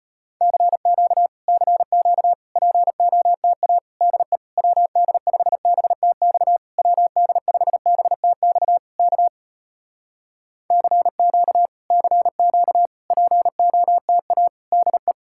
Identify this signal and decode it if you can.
Morse code, heard as a keyed tone, about 35 wpm, 700 Hz. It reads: CQ CQ POTA DE WD5BTX WD5BTX K CQ CQ POTA DE